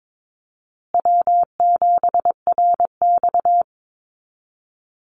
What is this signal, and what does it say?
Morse code, heard as a keyed tone, about 22 wpm, 700 Hz. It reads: W7RX
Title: Call Signs Repeated 2x - 22wpm